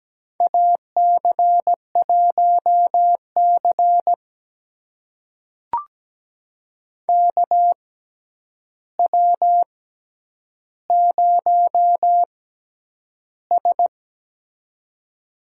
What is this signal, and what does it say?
Morse code, heard as a keyed tone, about 17 wpm, spaced as for 9 wpm, 700 Hz. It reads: AC1C KW0S
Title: USA Call Signs - 4 Characters - 6x Character Spacing - 17wpm